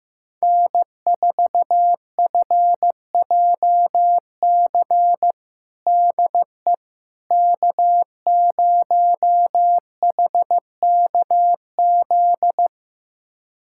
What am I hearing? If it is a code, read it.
Morse code, heard as a keyed tone, about 15 wpm, 700 Hz. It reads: N4FJC DE K0HKZ
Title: USA Callsign DE Callsign - 5 Characters - 15wpm